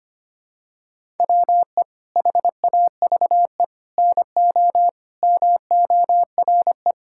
Morse code, heard as a keyed tone, about 25 wpm, 700 Hz. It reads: WE HAVE NO MORE
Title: Sentences from Top 1000 Words - Review Part 5 - 25wpm